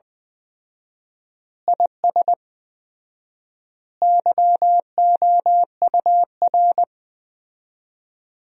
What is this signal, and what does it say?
Morse code, heard as a keyed tone, about 20 wpm, 700 Hz. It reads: IS YOUR
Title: 4x Word-Spacing - Sentences from Top 200 Words - 20wpm